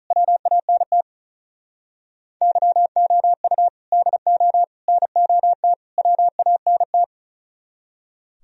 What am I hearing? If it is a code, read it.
Morse code, heard as a keyed tone, about 46 wpm, 700 Hz. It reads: WANT YOU DO NOT WANT